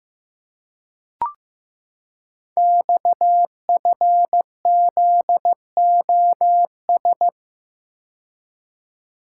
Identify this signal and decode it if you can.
Morse code, heard as a keyed tone, about 15 wpm, 700 Hz. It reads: XFZOS
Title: Code Groups - 5 Letters - 15wpm